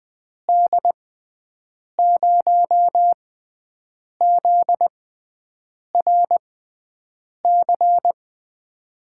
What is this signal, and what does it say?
Morse code, heard as a keyed tone, about 21 wpm, spaced as for 7 wpm, 700 Hz. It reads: D0ZRC
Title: USA Call Signs - 6 Characters - 6x Character Spacing - 20wpm